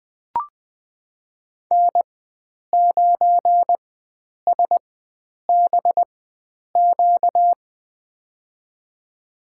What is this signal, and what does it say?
Morse code, heard as a keyed tone, about 20 wpm, spaced as for 9 wpm, 700 Hz. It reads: N9SBQ
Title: USA Call Signs - 4x Character Spacing - 20wpm